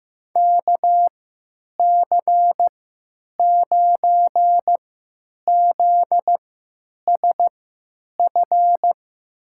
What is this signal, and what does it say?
Morse code, heard as a keyed tone, about 15 wpm, spaced as for 8 wpm, 700 Hz. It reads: KC9ZSF
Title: USA Call Signs - 3x Character Spacing - 15wpm